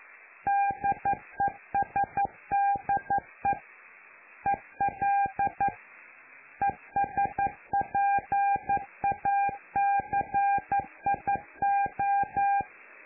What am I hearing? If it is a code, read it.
Morse code, heard as a keyed tone, about 11 wpm, 800 Hz. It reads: DESDE EL ESPACIO
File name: lu4aao / Globo Amsat